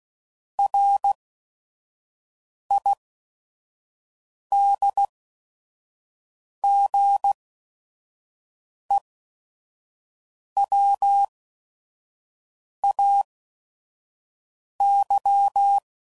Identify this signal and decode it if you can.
Morse code, heard as a keyed tone, about 16 wpm, spaced as for 5 wpm, 800 Hz. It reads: RIDGEWAY
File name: CW Code Groups @5WPM / fromSM's / Cities